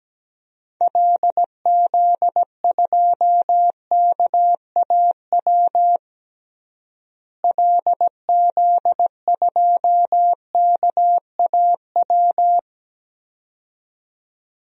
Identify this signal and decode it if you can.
Morse code, heard as a keyed tone, about 17 wpm, 700 Hz. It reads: LZ2KAW LZ2KAW
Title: DXCC-100 Call Signs - 6 Characters - Repeated 2x - 17wpm